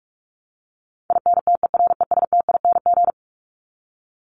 Morse code, heard as a keyed tone, about 44 wpm, 700 Hz. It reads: INTERESTING
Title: ING Suffix - Top 100 Words with Base Words 45wpm